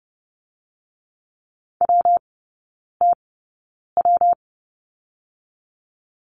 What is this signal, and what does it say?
Morse code, heard as a keyed tone, about 30 wpm, 700 Hz. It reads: W T W
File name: Lesson 19e - Letter w - Three Characters 30wpm